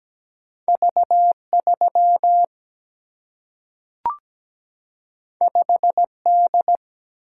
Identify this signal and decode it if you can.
Morse code, heard as a keyed tone, about 17 wpm, 700 Hz. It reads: V3 5D
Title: Single-Letter Single-Number 17wpm